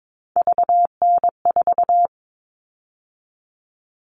Morse code, heard as a keyed tone, about 22 wpm, 700 Hz. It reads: VN4
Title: All Cut Numbers - 3-Digit 22wpm